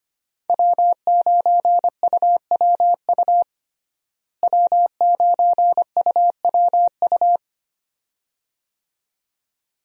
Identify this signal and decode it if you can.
Morse code, heard as a keyed tone, about 25 wpm, 700 Hz. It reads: W9UWU W9UWU
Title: USA Call Signs - 5 Characters - Repeated 2x - 25wpm